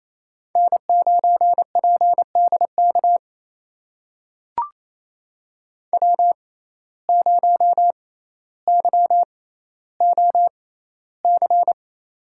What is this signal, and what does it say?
Morse code, heard as a keyed tone, about 28 wpm, spaced as for 14 wpm, 700 Hz. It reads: N9PDK W0YOC
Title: USA Call Signs - 6x Character Spacing - 28wpm